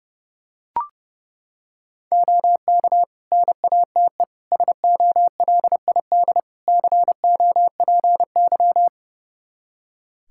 Morse code, heard as a keyed tone, about 30 wpm, 700 Hz. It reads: OK NATE SOLID COPY
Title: Practice QSO - Exchange 3 - More Info - 30wpm